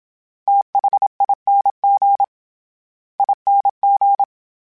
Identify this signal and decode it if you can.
Morse code, heard as a keyed tone, about 27 wpm, 800 Hz. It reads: THING ING
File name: Special INGs-mp3s / Special INGs-mp3s / SlowToFaster ing @25